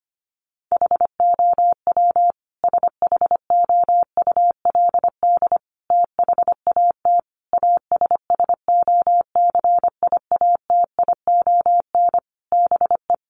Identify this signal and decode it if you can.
Morse code, heard as a keyed tone, about 25 wpm, 700 Hz. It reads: HOW SHOULD THAT ASSOCIATION BE
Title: Sentences from Top 1300 Words 25wpm